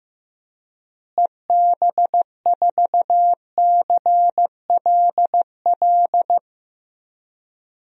Morse code, heard as a keyed tone, about 16 wpm, 700 Hz. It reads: EB4CLL
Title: DXCC-100 Call Signs - 6 Characters - Repeated 3x - 15wpm